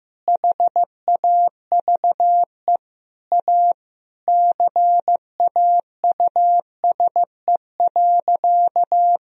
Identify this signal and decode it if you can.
Morse code, heard as a keyed tone, about 15 wpm, 700 Hz. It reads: HAVE A CAUSE.
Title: Sentences from Top 1000 Words - Review Part 1 - 15wpm